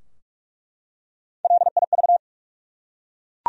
Morse code, heard as a keyed tone, about 47 wpm, 700 Hz. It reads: LIV
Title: USA Common Names - Max 5 Letters - 45wpm